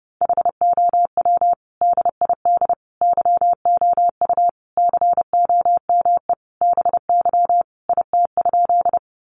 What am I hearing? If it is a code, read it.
Morse code, heard as a keyed tone, about 30 wpm, 700 Hz. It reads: HOW DID YOU COME BY IT?
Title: Sentences from Top 100 Words 30wpm